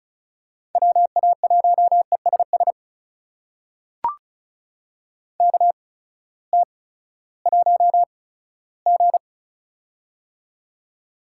Morse code, heard as a keyed tone, about 36 wpm, spaced as for 18 wpm, 700 Hz. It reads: WA1ESS KT1G
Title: USA Call Signs - 8x Character Spacing - 35wpm